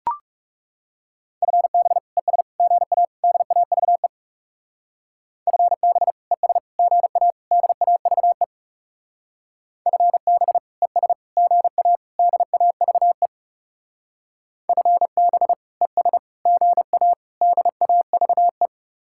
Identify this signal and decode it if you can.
Morse code, heard as a keyed tone, about 38 wpm, 700 Hz. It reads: FB ES GA DAVE FB ES GA DAVE FB ES GA DAVE FB ES GA DAVE